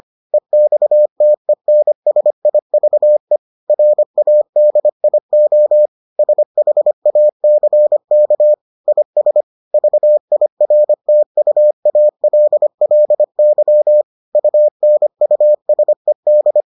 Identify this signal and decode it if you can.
Morse code, heard as a keyed tone, about 25 wpm, 600 Hz. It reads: EXTENSIVE RADIO SHACK IS VIRTUALLY UNUSED